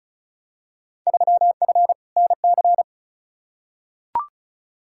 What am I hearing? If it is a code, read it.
Morse code, heard as a keyed tone, about 36 wpm, 700 Hz. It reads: 3F NC
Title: Field Day Exchanges - 35wpm